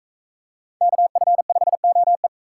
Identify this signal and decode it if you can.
Morse code, heard as a keyed tone, about 40 wpm, 700 Hz. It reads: KF5OE